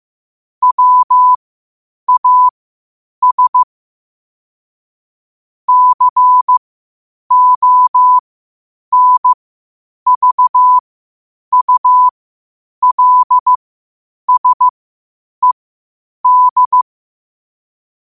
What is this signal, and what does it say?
Morse code, heard as a keyed tone, about 15 wpm, spaced as for 8 wpm, 1000 Hz. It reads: WAS CONVULSED
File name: qotdfiles / 8wpm